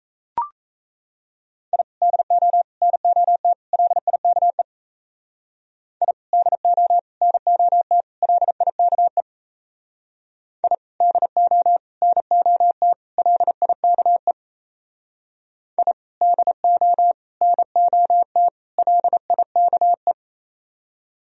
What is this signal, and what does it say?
Morse code, heard as a keyed tone, about 35 wpm, 700 Hz. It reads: I DO NOT LIKE I DO NOT LIKE I DO NOT LIKE I DO NOT LIKE
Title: Speed-Racing - Sets of 4 Words 28wpm